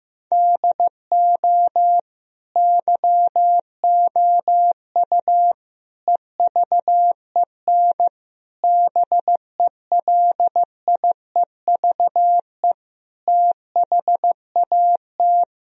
Morse code, heard as a keyed tone, about 15 wpm, 700 Hz. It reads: DO YOU EVEN BELIEVE THAT
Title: Sentences from Top 1000 Words - Review Part 1 - 15wpm